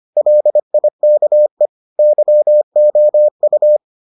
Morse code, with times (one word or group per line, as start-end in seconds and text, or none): 0.2-1.7 LIKE
2.0-3.8 YOU